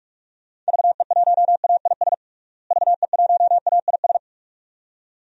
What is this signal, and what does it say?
Morse code, heard as a keyed tone, about 41 wpm, 700 Hz. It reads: VE1AIS VE1AIS
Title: DXCC-100 Call Signs - 6 Characters - Repeated 2x - 45wpm